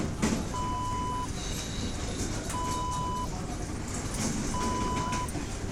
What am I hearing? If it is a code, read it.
There is no code.